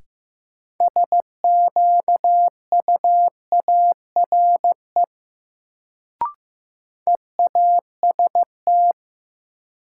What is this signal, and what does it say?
Morse code, heard as a keyed tone, about 15 wpm, 700 Hz. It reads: SQUARE EAST
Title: Sentences from Top 1200 Words - New Word Drill Alt - 15wpm